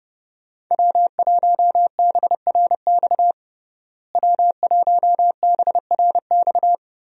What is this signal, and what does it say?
Morse code, heard as a keyed tone, about 30 wpm, 700 Hz. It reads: W1BRX W1BRX
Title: USA Call Signs - 5 Characters - Repeated 3x - 30wpm